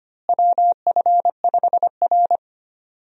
Morse code, heard as a keyed tone, about 25 wpm, 700 Hz. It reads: WF5R
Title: USA Call Signs - 4 Characters - Repeated 3x - 25wpm